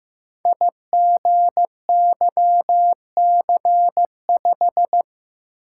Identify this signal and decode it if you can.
Morse code, heard as a keyed tone, about 15 wpm, 700 Hz. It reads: IGYC5
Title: Code Groups - 5 Letter:Numbers - 15wpm